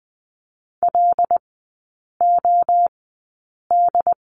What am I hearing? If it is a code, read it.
Morse code, heard as a keyed tone, about 20 wpm, spaced as for 8 wpm, 700 Hz. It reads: LOD
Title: 074 Lesson 13e - Letter L - Three Characters 20wpm